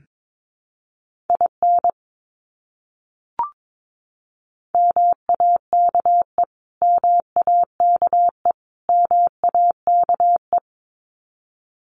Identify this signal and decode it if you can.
Morse code, heard as a keyed tone, about 22 wpm, 700 Hz. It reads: IN MAKE MAKE MAKE
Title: Top 300 Words Repeated 22wpm